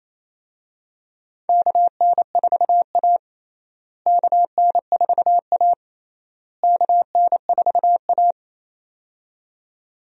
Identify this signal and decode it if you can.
Morse code, heard as a keyed tone, about 28 wpm, 700 Hz. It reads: KN4A KN4A KN4A
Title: USA Call Signs - 4 Characters - Repeated 3x - 28wpm